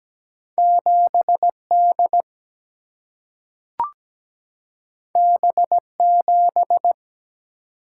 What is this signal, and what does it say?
Morse code, heard as a keyed tone, about 17 wpm, 700 Hz. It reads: 7D B7
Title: Single-Letter Single-Number 17wpm